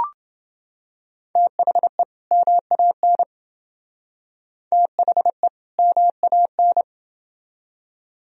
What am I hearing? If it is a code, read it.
Morse code, heard as a keyed tone, about 29 wpm, 700 Hz. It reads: THE MAN THE MAN